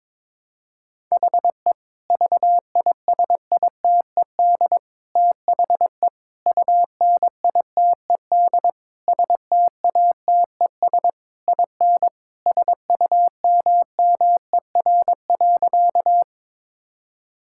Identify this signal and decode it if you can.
Morse code, heard as a keyed tone, about 22 wpm, 700 Hz. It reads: HE VISITED THE UNITED STATES IN SUMMER.
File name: practice / 22wpm / Sentences with 4-Word Phrases - 22wpm